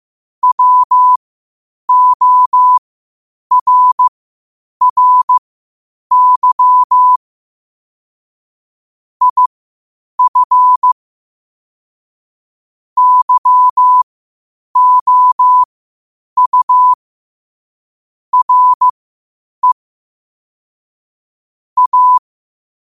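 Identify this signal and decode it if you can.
Morse code, heard as a keyed tone, about 15 wpm, spaced as for 8 wpm, 1000 Hz. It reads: WORRY IF YOU RE A
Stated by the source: Morse code at 8 wpm.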